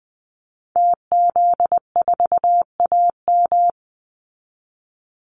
Morse code, heard as a keyed tone, about 20 wpm, 700 Hz. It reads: TZ4AM
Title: Call Signs - 20wpm